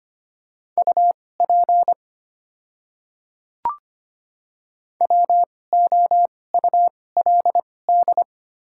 Morse code, heard as a keyed tone, about 25 wpm, spaced as for 18 wpm, 700 Hz. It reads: UP WOULD
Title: 2x Character Spacing - Top 300 Words - 25wpm